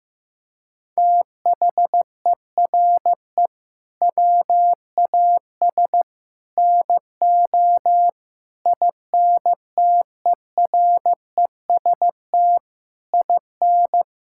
Morse code, heard as a keyed tone, about 15 wpm, 700 Hz. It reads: THERE WAS NO INTEREST IN
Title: Sentences from Top 1000 Words - Review Part 5 - 15wpm